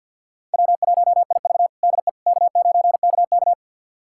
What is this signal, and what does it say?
Morse code, heard as a keyed tone, about 51 wpm, 700 Hz. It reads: W1IV DE K9KK
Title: USA Callsign DE Callsign - 4 Characters - 50wpm